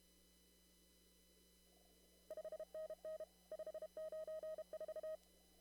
Morse code, heard as a keyed tone, about 32 wpm, 630 Hz. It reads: 5NN 594
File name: CW / kt5j wpx cw-op-k5pi